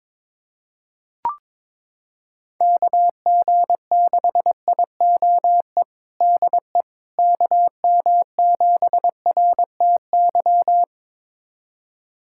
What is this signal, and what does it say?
Morse code, heard as a keyed tone, about 22 wpm, 700 Hz. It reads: KG6IOE DE KM7RTY
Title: USA Callsign DE Callsign 22wpm